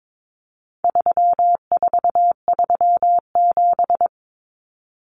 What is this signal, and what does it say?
Morse code, heard as a keyed tone, about 22 wpm, 700 Hz. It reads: 3437
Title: Common Cut Numbers - 4-Digit 22wpm